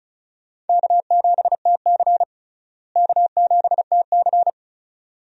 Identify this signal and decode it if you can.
Morse code, heard as a keyed tone, about 34 wpm, 700 Hz. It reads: K7TC K7TC
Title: USA Call Signs - 4 Characters - Repeated 2x - 35wpm